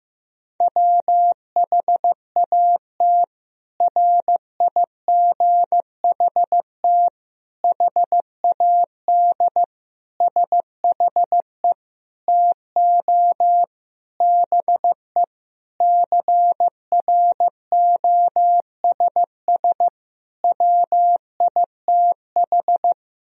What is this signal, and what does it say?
Morse code, heard as a keyed tone, about 16 wpm, 700 Hz. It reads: WHAT RIGHT HAD SHE TO BE CROSS WITH
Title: Sentences from Top 1000 Words - Review Part 1 - 15wpm